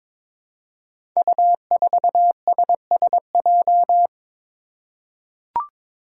Morse code, heard as a keyed tone, about 23 wpm, 700 Hz. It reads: U4SSJ
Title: Five Challenging Letter-Number Combinations 22wpm